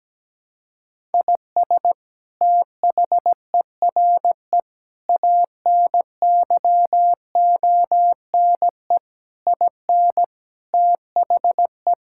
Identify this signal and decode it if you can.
Morse code, heard as a keyed tone, about 17 wpm, 700 Hz. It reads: IS THERE ANYONE IN THE
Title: Sentences from Top 1000 Words - Review Part 5 - 17wpm